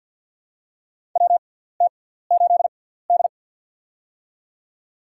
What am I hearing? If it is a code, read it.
Morse code, heard as a keyed tone, about 50 wpm, spaced as for 17 wpm, 700 Hz. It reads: WT8D
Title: USA Call Signs - 4 Characters - 6x Character Spacing - 50wpm